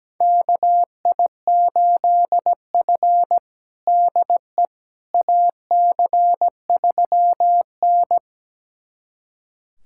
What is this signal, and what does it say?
Morse code, heard as a keyed tone, about 17 wpm, 700 Hz. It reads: KI8F DE AC3N
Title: USA Callsign DE Callsign - 4 Characters - 17wpm